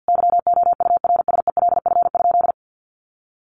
Morse code, heard as a keyed tone, about 50 wpm, 700 Hz. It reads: YOURSELF?
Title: Sentences from Top 1200 Words 50wpm